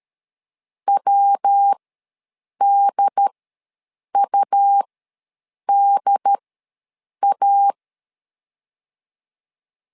Morse code, heard as a keyed tone, about 13 wpm, spaced as for 7 wpm, 800 Hz. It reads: WDUDA
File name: LEZIONI / 3 MWRO / 3 MW ws 12wpm 1816